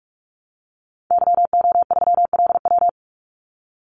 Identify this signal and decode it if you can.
Morse code, heard as a keyed tone, about 44 wpm, 700 Hz. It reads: YO3LW
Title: Call Signs Repeated 2x - 45wpm